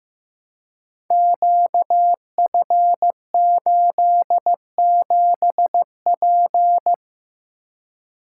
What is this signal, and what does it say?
Morse code, heard as a keyed tone, about 15 wpm, 700 Hz. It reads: QF87P
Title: Code Groups - 5 Letter:Numbers - 15wpm